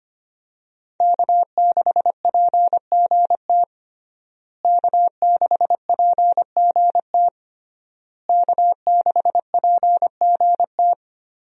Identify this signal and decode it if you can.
Morse code, heard as a keyed tone, about 25 wpm, 700 Hz. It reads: K6PGT K6PGT K6PGT
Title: USA Call Signs - 5 Characters - Repeated 3x - 25wpm